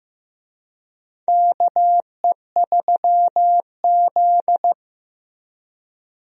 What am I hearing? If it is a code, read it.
Morse code, heard as a keyed tone, about 15 wpm, 700 Hz. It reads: KE3Z
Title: USA Call Signs - 4 Characters - Repeated 3x - 15wpm